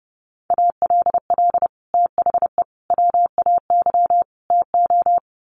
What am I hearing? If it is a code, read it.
Morse code, heard as a keyed tone, about 30 wpm, 700 Hz. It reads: ALL THE WAY TO
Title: Sets of 4 Words 30wpm